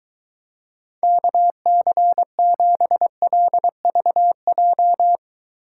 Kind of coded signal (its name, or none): Morse code